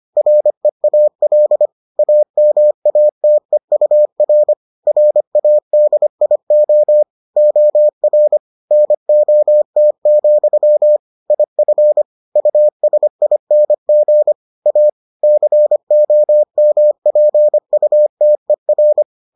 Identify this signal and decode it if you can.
Morse code, heard as a keyed tone, about 25 wpm, 600 Hz. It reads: REAL AMATEUR RADIO OR NOT, IF USING A COMPUTER